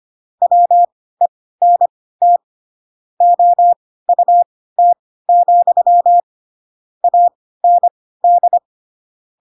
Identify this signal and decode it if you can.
Morse code, heard as a keyed tone, about 25 wpm, spaced as for 16 wpm, 700 Hz. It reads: WENT OUT, AND
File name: Pinocchio / Pinocchio ch7 10 25 ik1hge